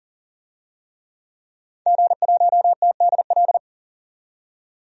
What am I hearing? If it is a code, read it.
Morse code, heard as a keyed tone, about 40 wpm, 700 Hz. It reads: G1TDL
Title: DXCC-100 Call Signs - 5 Characters - 40wpm